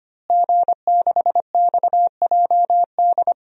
Morse code, heard as a keyed tone, about 25 wpm, 700 Hz. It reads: G6XJD
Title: DXCC-100 Call Signs - 5 Characters - Repeated 2x - 25wpm